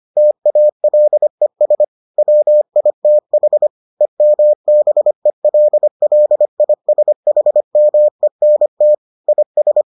Morse code, heard as a keyed tone, about 25 wpm, 600 Hz. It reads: TALES WITH EMBELLISHMENT IS